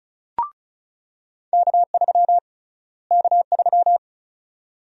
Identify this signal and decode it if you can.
Morse code, heard as a keyed tone, about 36 wpm, 700 Hz. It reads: K3 K3
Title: USA Call Sign Prefixes - Repeated 2x - 35wpm